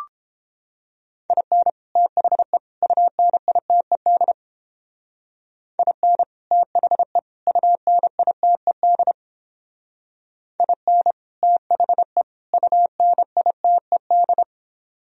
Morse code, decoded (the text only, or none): IN THE UNITED IN THE UNITED IN THE UNITED